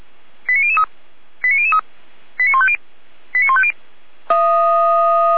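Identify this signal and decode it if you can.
Five-tone selective call (ZVEI), heard as quick runs of five tones, tones of about 70 ms. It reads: 89003 89003 89150 89150
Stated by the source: Five-tone ZVEI selective call received by SDR.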